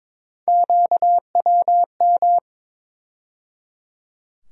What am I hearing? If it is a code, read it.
Morse code, heard as a keyed tone, about 21 wpm, 700 Hz. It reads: QWM
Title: USA Call Sign Postfixes - 3 Letters 22wpm